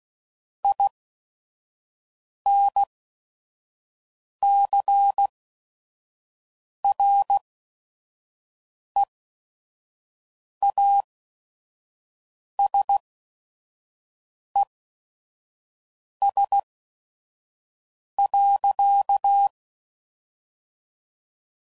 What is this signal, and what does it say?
Morse code, heard as a keyed tone, about 16 wpm, spaced as for 5 wpm, 800 Hz. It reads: INCREASES.
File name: CW Code Groups @5WPM / Technical Subjects / XL